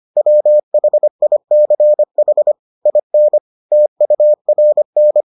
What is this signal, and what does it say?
Morse code, heard as a keyed tone, about 25 wpm, 600 Hz. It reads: WHICH IN TURN